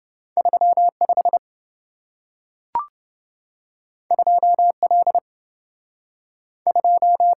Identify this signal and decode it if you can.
Morse code, heard as a keyed tone, about 27 wpm, 700 Hz. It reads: 35 2L 2